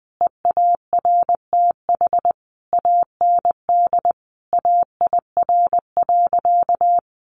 Morse code, heard as a keyed tone, about 20 wpm, 700 Hz. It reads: EARTH AND AIR.
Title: Sentences from Top 400 Words 20wpm